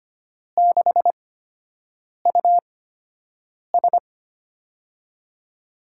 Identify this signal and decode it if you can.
Morse code, heard as a keyed tone, about 25 wpm, spaced as for 7 wpm, 700 Hz. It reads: 6US
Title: USA Call Signs - 5 Characters - 8x Character Spacing - 25wpm